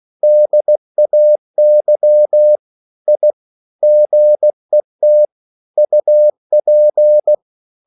Morse code, heard as a keyed tone, about 16 wpm, 600 Hz. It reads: DAY I GET UP